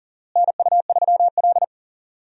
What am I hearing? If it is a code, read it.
Morse code, heard as a keyed tone, about 37 wpm, 700 Hz. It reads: NU3L
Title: USA Call Signs - 4 Characters - Repeated 3x - 40wpm